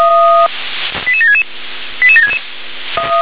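Five-tone selective call (ZVEI), heard as quick runs of five tones, tones of about 70 ms. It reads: 88766 88766
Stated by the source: ZVEI five-tone selective call 88725 received on SDR.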